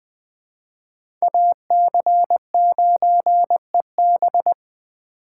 Morse code, heard as a keyed tone, about 20 wpm, 700 Hz. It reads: AC9EB
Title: Speed Racing - USA Call Signs 17wpm